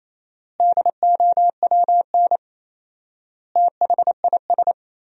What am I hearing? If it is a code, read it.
Morse code, heard as a keyed tone, about 28 wpm, 700 Hz. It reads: DOWN THIS
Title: 4x Word-Spacing - Sentences from Top 400 Words - 28wpm